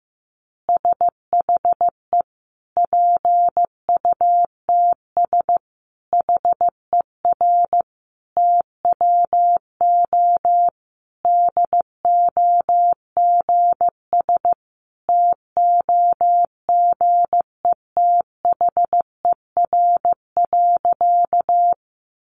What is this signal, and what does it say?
Morse code, heard as a keyed tone, about 15 wpm, 700 Hz. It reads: SHE PUTS HER TWO DOGS TOGETHER.
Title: Easy Phrases 15wpm